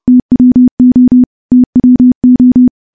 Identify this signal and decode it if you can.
Morse code, heard as a keyed tone, about 30 wpm, 250 Hz. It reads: TWO TWO
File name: morse / morse-training-30WPM